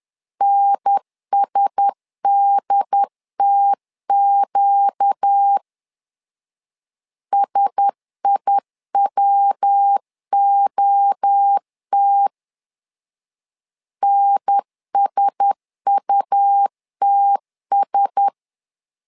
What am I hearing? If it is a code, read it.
Morse code, heard as a keyed tone, about 10 wpm, 800 Hz. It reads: NSDTQ SIWOT NSUTS
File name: LEZIONI / 4 Q1S 10wpm 2133